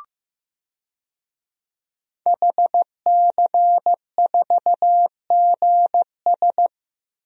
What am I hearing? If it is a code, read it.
Morse code, heard as a keyed tone, about 15 wpm, 700 Hz. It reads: HC4GS
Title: DXCC-100 Call Signs - 5 Characters - Repeated 3x - 15wpm